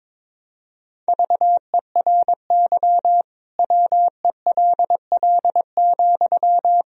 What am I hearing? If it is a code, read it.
Morse code, heard as a keyed tone, about 22 wpm, 700 Hz. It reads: VERY WELL,
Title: Sentences from Top 1000 Words - Review Part 4 - 22wpm